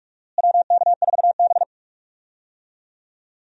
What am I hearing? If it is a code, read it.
Morse code, heard as a keyed tone, about 47 wpm, 700 Hz. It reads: WK4B
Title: USA Call Signs - 4 Characters - Repeated 3x - 45wpm